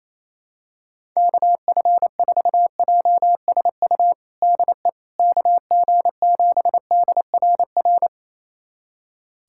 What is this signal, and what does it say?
Morse code, heard as a keyed tone, about 27 wpm, 700 Hz. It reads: KF4JSU DE KG7DRR
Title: USA Callsign DE Callsign 28wpm